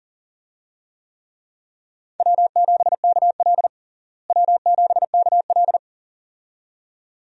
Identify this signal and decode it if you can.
Morse code, heard as a keyed tone, about 40 wpm, 700 Hz. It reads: W7KL W7KL
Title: USA Call Signs - 4 Characters - Repeated 2x - 40wpm